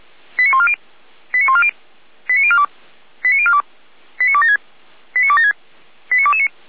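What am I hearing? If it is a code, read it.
Five-tone selective call (ZVEI), heard as quick runs of five tones, tones of about 80 ms. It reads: 89140 89140 89042 89042 89276 89276 89229
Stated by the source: SDR-received ZVEI five-tone selective call, code 89250, logged by feuerwehr-orlamuende.org fire brigade.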